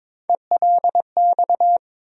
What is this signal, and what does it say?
Morse code, heard as a keyed tone, about 22 wpm, 700 Hz. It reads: ELX
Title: USA Call Sign Postfixes - 3 Letters 22wpm